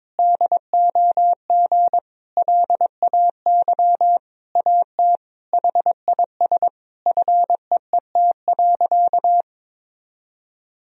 Morse code, heard as a keyed tone, about 22 wpm, 700 Hz. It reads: DOG LAY AT HIS FEET.
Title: Sentences from Top 1000 Words - Review Part 2 - 22wpm